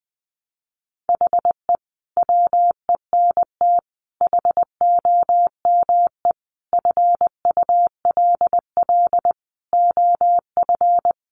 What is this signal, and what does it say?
Morse code, heard as a keyed tone, about 20 wpm, 700 Hz. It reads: HE WENT HOME FULL OF
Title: Sentences from Top 1300 Words 20wpm